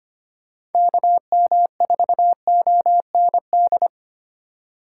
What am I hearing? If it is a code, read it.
Morse code, heard as a keyed tone, about 25 wpm, 700 Hz. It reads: KM4OND
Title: USA Call Signs - 6 Characters - 3x Character Spacing - 25wpm